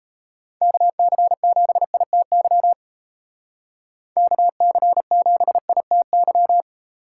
Morse code, heard as a keyed tone, about 34 wpm, 700 Hz. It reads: KC7ITY KC7ITY
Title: Speed Racing - USA 2x3 Call Signs 28wpm